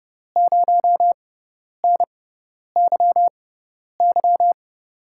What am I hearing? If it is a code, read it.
Morse code, heard as a keyed tone, about 30 wpm, spaced as for 10 wpm, 700 Hz. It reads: 0NYY
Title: USA Call Signs - 5 Characters - 6x Character Spacing - 30wpm